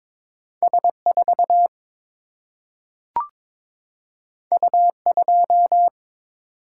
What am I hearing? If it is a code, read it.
Morse code, heard as a keyed tone, about 22 wpm, 700 Hz. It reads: S4 U2
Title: Single-Letter Single-Number 22wpm